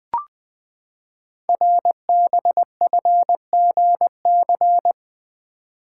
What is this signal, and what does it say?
Morse code, heard as a keyed tone, about 20 wpm, 700 Hz. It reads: RBFGC
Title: Code Groups - 5 Letters - 20wpm